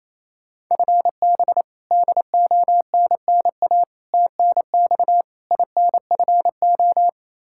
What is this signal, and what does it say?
Morse code, heard as a keyed tone, about 28 wpm, 700 Hz. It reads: FB DONNA TNX INFO
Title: Practice QSO - Exchange 3 - More Info - 28wpm